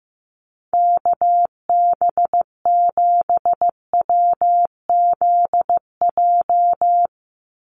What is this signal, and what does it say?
Morse code, heard as a keyed tone, about 15 wpm, 700 Hz. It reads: KB7WZJ
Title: USA Call Signs 15wpm